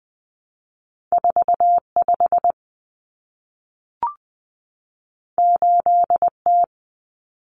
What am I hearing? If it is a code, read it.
Morse code, heard as a keyed tone, about 20 wpm, 700 Hz. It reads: 45 8T
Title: Common Cut Numbers - 2-Digit 20wpm